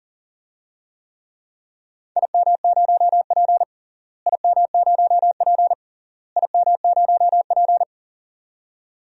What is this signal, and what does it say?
Morse code, heard as a keyed tone, about 40 wpm, 700 Hz. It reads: IM0P IM0P IM0P
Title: DXCC-100 Call Signs - 4 Characters - Repeated 3x - 40wpm